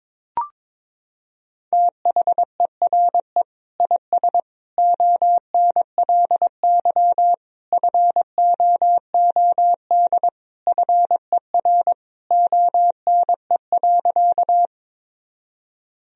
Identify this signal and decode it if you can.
Morse code, heard as a keyed tone, about 22 wpm, 700 Hz. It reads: THERE IS ONLY FOOD FER ONE.
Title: Sentences from Top 300 Words and Abbreviations 22wpm